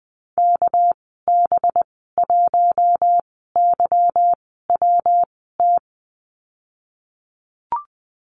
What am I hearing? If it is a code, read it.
Morse code, heard as a keyed tone, about 20 wpm, spaced as for 15 wpm, 700 Hz. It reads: KB1YWT
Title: 219 Lesson 35g - Number 0 - Call Signs 20wpm